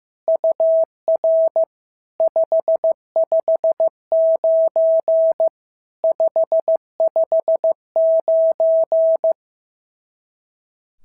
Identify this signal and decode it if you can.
Morse code, heard as a keyed tone, about 15 wpm, 650 Hz. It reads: UR 559 559
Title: SOTA QSO Practice 15wpm